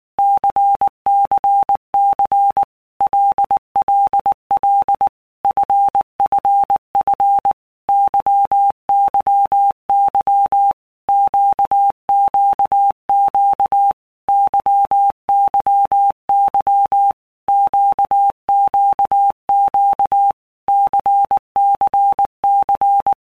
Morse code, heard as a keyed tone, about 19 wpm, 800 Hz. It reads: CCC LLL FFF YYY QQQ YYY QQQ CCC
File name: Basic Letter Groups By Blocks / Groups@18 / LFCYQ@18